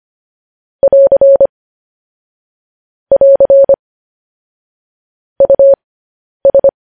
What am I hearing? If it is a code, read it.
Morse code, heard as a keyed tone, about 25 wpm, spaced as for 10 wpm, 550 Hz. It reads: <AR> <AR> US
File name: podcast / News headlines-1x-25x10